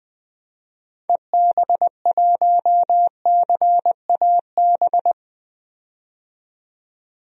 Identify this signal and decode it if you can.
Morse code, heard as a keyed tone, about 20 wpm, 700 Hz. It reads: EB1CAB
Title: DXCC-100 Call Signs - 6 Characters - 20wpm